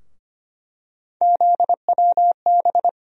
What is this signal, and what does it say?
Morse code, heard as a keyed tone, about 25 wpm, 700 Hz. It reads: ZWB
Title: USA Call Sign Postfixes - 3 Letters 25wpm